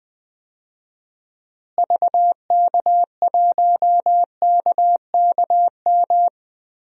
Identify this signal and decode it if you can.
Morse code, heard as a keyed tone, about 20 wpm, 700 Hz. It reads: VK1KKM
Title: DXCC-100 Call Signs - 6 Characters - Repeated 3x - 20wpm